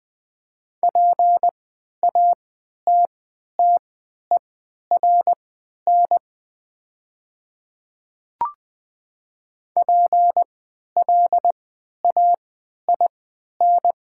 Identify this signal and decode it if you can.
Morse code, heard as a keyed tone, about 20 wpm, spaced as for 11 wpm, 700 Hz. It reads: PATTERN PLAIN
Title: 3x Character Spacing - Top 500 Words - 20wpm